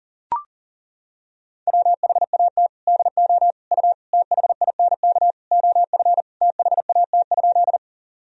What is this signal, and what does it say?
Morse code, heard as a keyed tone, about 40 wpm, 700 Hz. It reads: WHAT DO U THINK OF THAT?
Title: Sentences from Top 200 Words and Abbreviations 40wpm